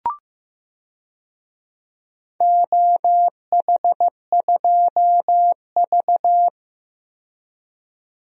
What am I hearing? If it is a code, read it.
Morse code, heard as a keyed tone, about 15 wpm, 700 Hz. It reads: OH2V
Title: DXCC-100 Call Signs - 4 Characters - 15wpm